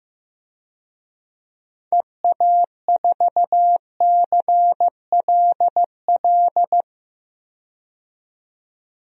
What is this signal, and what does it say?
Morse code, heard as a keyed tone, about 16 wpm, 700 Hz. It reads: EA4CLL
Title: DXCC-100 Call Signs - 6 Characters - 15wpm